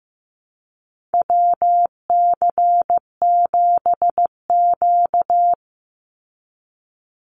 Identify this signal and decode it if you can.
Morse code, heard as a keyed tone, about 15 wpm, 700 Hz. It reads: WC7Q
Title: Call Signs Repeated 2x - 15wpm